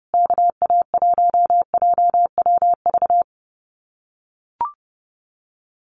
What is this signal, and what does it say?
Morse code, heard as a keyed tone, about 30 wpm, 700 Hz. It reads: KA1JWV
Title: Common QSO Phrases Encore 30wpm